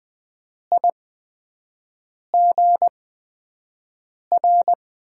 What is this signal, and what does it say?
Morse code, heard as a keyed tone, about 20 wpm, spaced as for 5 wpm, 700 Hz. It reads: IGR
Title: USA Call Signs - 6 Characters - 8x Character Spacing - 20wpm